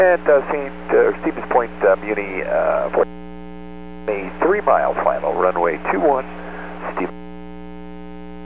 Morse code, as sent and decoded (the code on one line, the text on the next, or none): none